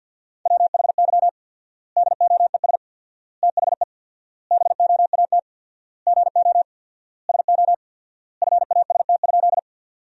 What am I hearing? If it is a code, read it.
Morse code, heard as a keyed tone, about 51 wpm, 700 Hz. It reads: WHY DOES THE BOAT GO SO FAST?
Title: 4x Word-Spacing - Sentences from Top 500 Words - 50wpm